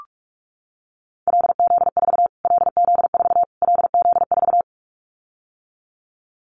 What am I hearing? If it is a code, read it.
Morse code, heard as a keyed tone, about 44 wpm, 700 Hz. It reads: LZ4 LZ4 LZ4
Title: Callsign Prefixes for Top 25 DXCC - Repeated 3x - 45wpm